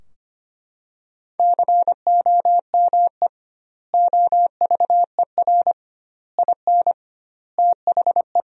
Morse code, heard as a keyed tone, about 25 wpm, 700 Hz. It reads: COME OVER IN THE
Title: 2x Word-Spacing - Sentences from Top 300 Words - 25wpm